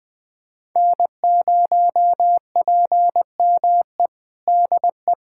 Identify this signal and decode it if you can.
Morse code, heard as a keyed tone, about 20 wpm, 700 Hz. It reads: N0PME DE
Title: USA Callsign DE Callsign - 5 Characters - 20wpm